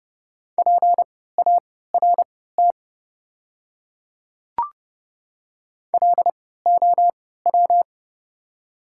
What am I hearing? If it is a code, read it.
Morse code, heard as a keyed tone, about 30 wpm, spaced as for 17 wpm, 700 Hz. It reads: PART LOW
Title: 3x Character Spacing - Top 300 Words - 30wpm